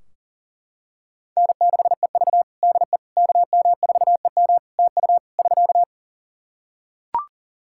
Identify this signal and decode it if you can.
Morse code, heard as a keyed tone, about 40 wpm, 700 Hz. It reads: N6EV DE KM4EM TU <SK>
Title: Practice QSO - Exchange 4 - Ending - 40wpm